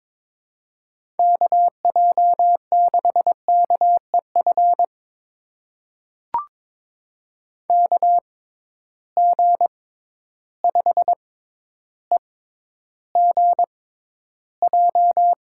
Morse code, heard as a keyed tone, about 22 wpm, spaced as for 11 wpm, 700 Hz. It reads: KJ6KEF KG5EGJ
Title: USA Call Signs - 6 Characters - 6x Character Spacing - 22wpm